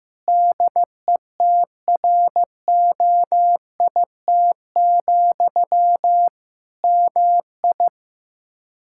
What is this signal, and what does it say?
Morse code, heard as a keyed tone, about 15 wpm, 700 Hz. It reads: DETROIT, MI
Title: K1USN SST QSO Practice 15wpm